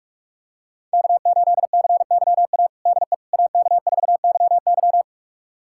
Morse code, heard as a keyed tone, about 46 wpm, 700 Hz. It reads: K8CYA DE AK4YY
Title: USA Callsign DE Callsign - 5 Characters - 45wpm